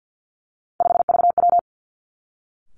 Morse code, heard as a keyed tone, about 50 wpm, 700 Hz. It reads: 5VW
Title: Three Challenging Letter-Number Combinations 50wpm